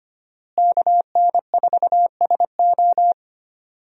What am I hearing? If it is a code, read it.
Morse code, heard as a keyed tone, about 25 wpm, 700 Hz. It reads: KN4SO
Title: USA Call Signs - 5 Characters - Repeated 3x - 25wpm